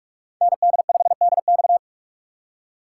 Morse code, heard as a keyed tone, about 54 wpm, 700 Hz. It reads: ND5DX
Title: Speed Racing - USA 2x2 Call Signs 45wpm